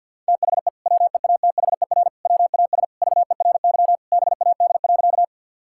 Morse code, heard as a keyed tone, about 50 wpm, 700 Hz. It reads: THE WEATHER WAS VERY BAD.
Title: Sentences from Top 1000 Words - Review Part 5 - 50wpm